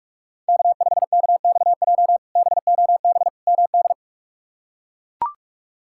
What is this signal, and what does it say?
Morse code, heard as a keyed tone, about 42 wpm, 700 Hz. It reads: K5KXJ BOB MD
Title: K1USN SST QSO Practice - 45wpm